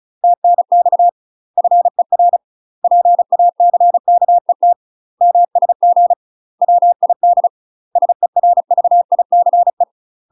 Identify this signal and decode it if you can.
Morse code, heard as a keyed tone, about 36 wpm, 700 Hz. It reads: TNX FER PACKET MSG WID SERVICE
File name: UK MASTER QSOs TRAINING / 35WPM / UK MASTER QSOs 0029